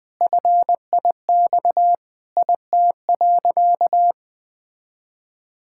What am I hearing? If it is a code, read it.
Morse code, heard as a keyed tone, about 20 wpm, 700 Hz. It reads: FIX IT.
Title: Sentences with 7-Word Phrases - 20wpm